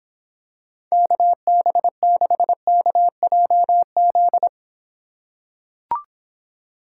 Morse code, heard as a keyed tone, about 26 wpm, 700 Hz. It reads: KB6KJZ